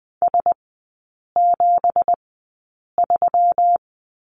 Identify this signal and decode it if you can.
Morse code, heard as a keyed tone, about 20 wpm, spaced as for 8 wpm, 690 Hz. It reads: S73
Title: 166 Lesson 27e - Number 7 - Three Characters 20wpm